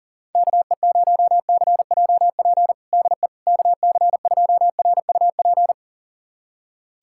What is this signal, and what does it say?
Morse code, heard as a keyed tone, about 40 wpm, 700 Hz. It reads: KE0CJP DE KC2RUP
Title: USA Callsign DE Callsign - 6 Characters - 40wpm